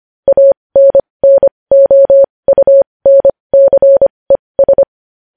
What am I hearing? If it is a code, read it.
Morse code, heard as a keyed tone, about 25 wpm, spaced as for 20 wpm, 550 Hz. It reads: ANNOUNCES
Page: news-headlines-in-morse-code-at-20-wpm